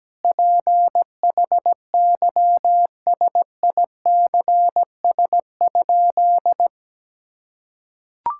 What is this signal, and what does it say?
Morse code, heard as a keyed tone, about 17 wpm, 700 Hz. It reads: PHYSICS?
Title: Sentences from Top 1000 Words - Review Part 6 - 17wpm